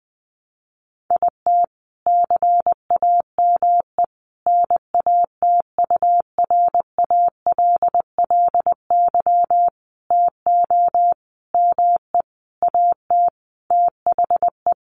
Morse code, heard as a keyed tone, about 19 wpm, 700 Hz. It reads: IT CAME NATURALLY TO ME AT THE
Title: Sentences from Top 1300 Words 20wpm